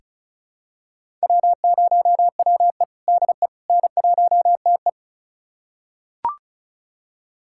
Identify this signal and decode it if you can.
Morse code, heard as a keyed tone, about 36 wpm, 700 Hz. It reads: W0WE DE N1TE
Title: USA Callsign DE Callsign - 4 Characters - 35wpm